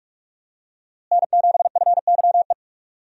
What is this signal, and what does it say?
Morse code, heard as a keyed tone, about 44 wpm, 700 Hz. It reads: N7FYE
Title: USA Call Signs - 5 Characters - Repeated 3x - 45wpm